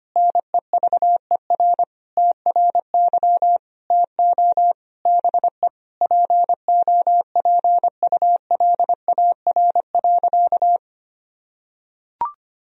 Morse code, heard as a keyed tone, about 25 wpm, 700 Hz. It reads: NEVER TRY TO BE POPULAR.
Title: Sentences from Top 1000 Words - Review Part 3 - 25wpm